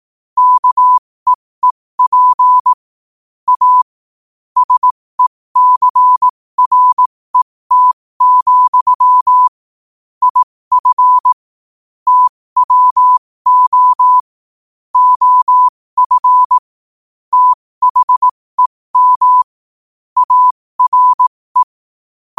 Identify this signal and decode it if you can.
Morse code, heard as a keyed tone, about 18 wpm, spaced as for 15 wpm, 1000 Hz. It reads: KEEP A SECRET, IF TWO OF THEM ARE